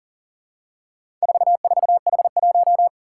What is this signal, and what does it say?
Morse code, heard as a keyed tone, about 40 wpm, 700 Hz. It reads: 44H1
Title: Four Challenging Letter-Number Combinations 40wpm